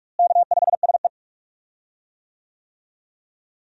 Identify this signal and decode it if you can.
Morse code, heard as a keyed tone, about 46 wpm, 700 Hz. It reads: K5SE
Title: USA Call Signs - 4 Characters - Repeated 3x - 45wpm